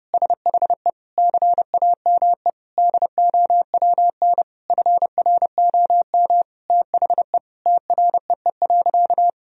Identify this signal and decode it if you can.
Morse code, heard as a keyed tone, about 32 wpm, 700 Hz. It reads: SHE CAME DOWN FROM THE TREE.
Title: Sentences from Top 1000 Words - Review Part 1 - 30wpm